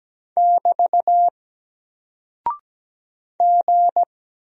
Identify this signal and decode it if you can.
Morse code, heard as a keyed tone, about 17 wpm, 700 Hz. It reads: <BT> G
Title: Single Character (Letter:Number:Punctuation:Prosign) 17wpm